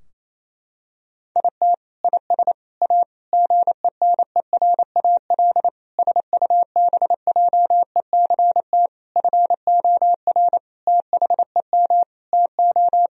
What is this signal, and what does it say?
Morse code, heard as a keyed tone, about 28 wpm, 690 Hz. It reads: IT IS A GENERAL SUBJECT FOR THEM TO
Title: Sentences from Top 1000 Words - Review Part 3 - 28wpm